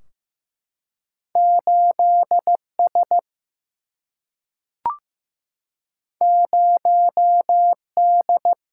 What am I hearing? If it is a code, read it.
Morse code, heard as a keyed tone, about 15 wpm, 700 Hz. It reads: 8 S 0 D
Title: Single-Letter Single-Number 15wpm